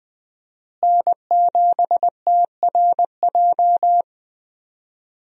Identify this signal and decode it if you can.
Morse code, heard as a keyed tone, about 20 wpm, 700 Hz. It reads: N7TRJ
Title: USA Call Signs - 5 Characters - Repeated 3x - 20wpm